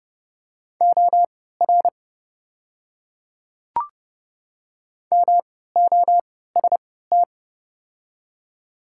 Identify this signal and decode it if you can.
Morse code, heard as a keyed tone, about 32 wpm, spaced as for 17 wpm, 700 Hz. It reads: OR MOST
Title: 3x Character Spacing - Top 100 Words - 30wpm